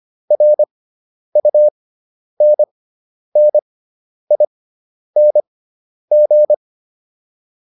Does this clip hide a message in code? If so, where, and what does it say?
Yes: Morse code from 0.3 to 6.5 s, heard as a keyed tone, about 25 wpm, spaced as for 10 wpm, 600 Hz. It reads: RUNNING